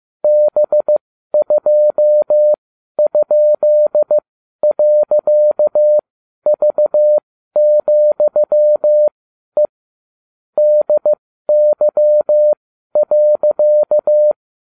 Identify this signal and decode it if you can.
Morse code, heard as a keyed tone, about 15 wpm, spaced as for 12 wpm, 600 Hz. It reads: B2?.V,E DY.